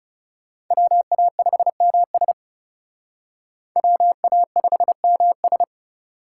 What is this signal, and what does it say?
Morse code, heard as a keyed tone, about 36 wpm, 700 Hz. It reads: WA5MS WA5MS
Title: Speed Racing - USA 2x2 Call Signs 30wpm